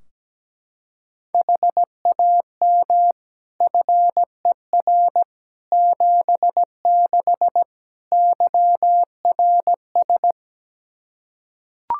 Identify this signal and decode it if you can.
Morse code, heard as a keyed tone, about 17 wpm, 700 Hz. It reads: HAM FER 76 YRS
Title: Practice QSO - Exchange 3 - More Info - 17wpm